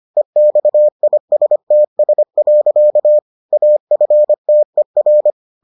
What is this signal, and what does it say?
Morse code, heard as a keyed tone, about 25 wpm, 600 Hz. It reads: EXISTS. AFTER